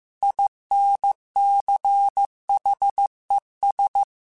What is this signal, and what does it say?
Morse code, heard as a keyed tone, about 15 wpm, 800 Hz. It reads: INCHES
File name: QSOs / QSOs-mp3s / WA7QOX 1@13